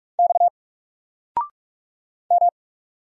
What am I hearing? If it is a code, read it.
Morse code, heard as a keyed tone, about 45 wpm, 700 Hz. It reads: X M
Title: Single Letter-Number 45wpm